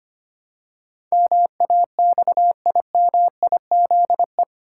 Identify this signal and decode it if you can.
Morse code, heard as a keyed tone, about 25 wpm, 700 Hz. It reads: MAXIMIZE
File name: 8 Letter Words - First 3 Letters Given - 25wpm